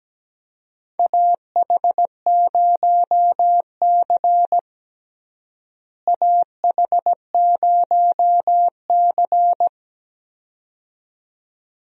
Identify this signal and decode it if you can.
Morse code, heard as a keyed tone, about 17 wpm, 700 Hz. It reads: AH0C AH0C
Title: USA Call Signs - 4 Characters - Repeated 2x - 17wpm